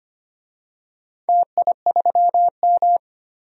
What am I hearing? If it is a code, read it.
Morse code, heard as a keyed tone, about 25 wpm, 700 Hz. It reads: TI3M
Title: DXCC-100 Call Signs - 4 Characters - 25wpm